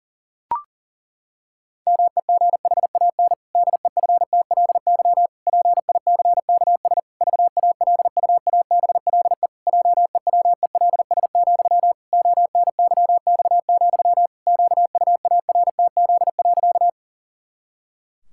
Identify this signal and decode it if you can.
Morse code, heard as a keyed tone, about 40 wpm, 700 Hz. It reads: MEGHAN DEFTLY PICKS VALUABLE JEWELS, ONYX, QUARTZ.
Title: Pangram Challenge v2 - 40wpm